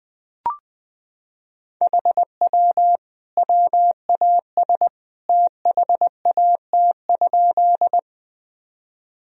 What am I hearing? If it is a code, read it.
Morse code, heard as a keyed tone, about 20 wpm, 700 Hz. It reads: HW WAS THAT?
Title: Sentences from Top 100 Words and Abbreviations 20wpm